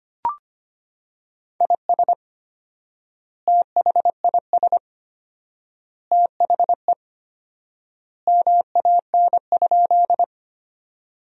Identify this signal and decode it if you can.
Morse code, heard as a keyed tone, about 25 wpm, 700 Hz. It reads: IS THIS THE MAN?
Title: 4x Word-Spacing - Sentences from Top 200 Words - 25wpm